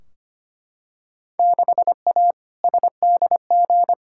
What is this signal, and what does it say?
Morse code, heard as a keyed tone, about 25 wpm, 700 Hz. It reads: 6A SDG
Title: Field Day Exchanges Encore - 25wpm